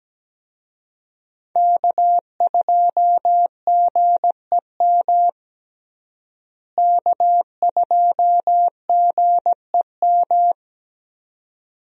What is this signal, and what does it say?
Morse code, heard as a keyed tone, about 17 wpm, 700 Hz. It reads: K2GEM K2GEM
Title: USA Call Signs - 5 Characters - Repeated 2x - 17wpm